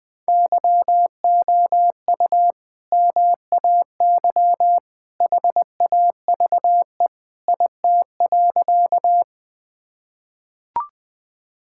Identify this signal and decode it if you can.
Morse code, heard as a keyed tone, about 20 wpm, 700 Hz. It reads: YOU MAY HAVE IT.
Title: Sentences with 2-Word Phrases - 20wpm